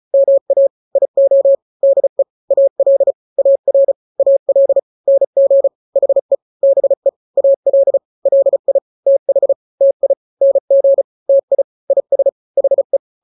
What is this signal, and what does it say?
Morse code, heard as a keyed tone, about 34 wpm, 550 Hz. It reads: MA IO DE AL AR AL NG HE BE AL LI TH TI NG TI IS HE
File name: n-grams / 2-grams 35 30 WPM